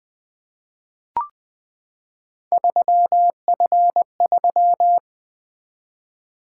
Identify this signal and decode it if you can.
Morse code, heard as a keyed tone, about 20 wpm, 700 Hz. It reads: 3F3
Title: Three Letter-Number Combinations - 20wpm